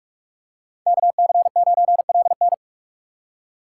Morse code, heard as a keyed tone, about 47 wpm, 690 Hz. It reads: KC9LN
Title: USA Call Signs - 5 Characters - 2x Character Spacing - 45wpm